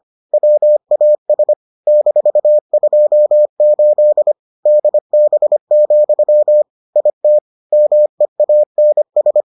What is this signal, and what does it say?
Morse code, heard as a keyed tone, about 25 wpm, 600 Hz. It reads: WAS -28 DB, IT MEANS